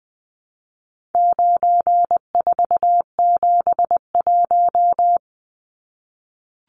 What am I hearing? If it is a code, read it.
Morse code, heard as a keyed tone, about 20 wpm, 700 Hz. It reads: 9471
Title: Numbers - 4-Digit 20wpm